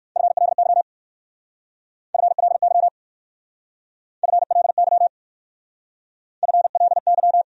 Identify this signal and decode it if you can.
Morse code, heard as a keyed tone, about 55 wpm, 700 Hz. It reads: FLY FLY FLY FLY
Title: Speed-Racing Top 500 Words 45wpm